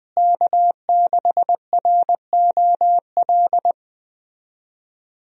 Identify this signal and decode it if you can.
Morse code, heard as a keyed tone, about 20 wpm, 700 Hz. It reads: K6ROL
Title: USA Call Signs - 5 Characters - Repeated 2x - 20wpm